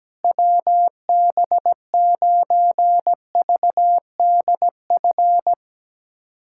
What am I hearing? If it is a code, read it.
Morse code, heard as a keyed tone, about 17 wpm, 700 Hz. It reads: WB9VDF
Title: USA Callsign DE Callsign - 6 Characters - 17wpm